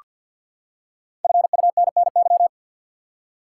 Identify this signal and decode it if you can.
Morse code, heard as a keyed tone, about 47 wpm, 700 Hz. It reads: FUNNY
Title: Five Letter Words - 50wpm